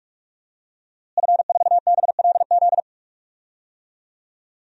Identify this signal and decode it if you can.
Morse code, heard as a keyed tone, about 46 wpm, 700 Hz. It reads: F4BLZ
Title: DXCC-100 Call Signs - 5 Characters - Repeated 3x - 45wpm